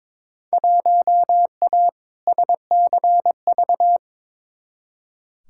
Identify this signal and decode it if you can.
Morse code, heard as a keyed tone, about 22 wpm, 700 Hz. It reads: 1A SCV
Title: Field Day Exchanges - 22wpm